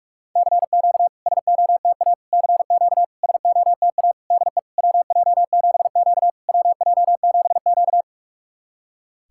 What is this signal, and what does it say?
Morse code, heard as a keyed tone, about 44 wpm, 690 Hz. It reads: CQ SOTA CQ SOTA DE WJ7Q WJ7Q
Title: SOTA QSO Practice 45wpm